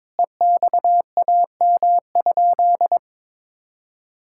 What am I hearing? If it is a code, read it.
Morse code, heard as a keyed tone, about 22 wpm, 700 Hz. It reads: EXAM?
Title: Sentences from Top 1000 Words - Review Part 5 - 22wpm